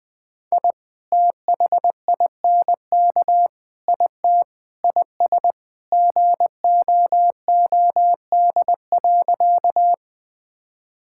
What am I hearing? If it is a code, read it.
Morse code, heard as a keyed tone, about 21 wpm, 700 Hz. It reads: I THINK IT IS GOOD.
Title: Sentences with 3-Word Phrases - 20wpm